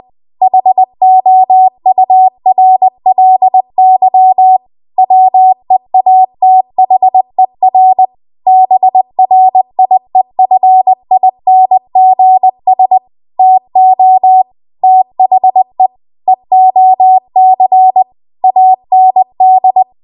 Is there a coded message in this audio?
Yes: Morse code from 0.4 to 19.8 s, heard as a keyed tone, about 20 wpm, 750 Hz. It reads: HOURLY WEATHER BRIEFINGS TO THE EOC AND